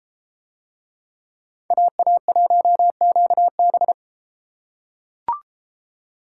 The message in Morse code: AA1QB

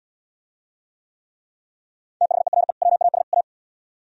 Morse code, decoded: THERE WAS A

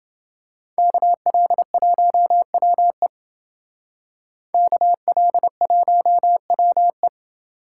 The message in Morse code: KL1WE KL1WE

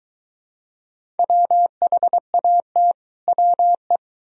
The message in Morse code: WHAT WE

